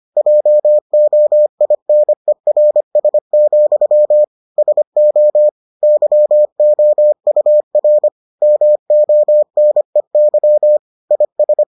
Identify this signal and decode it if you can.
Morse code, heard as a keyed tone, about 25 wpm, 600 Hz. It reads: JOINERS, SO YOUR MONEY IS